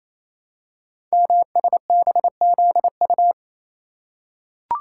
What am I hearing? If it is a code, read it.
Morse code, heard as a keyed tone, about 27 wpm, 700 Hz. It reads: MSBZU
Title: Code Groups - 5 Letters - 28wpm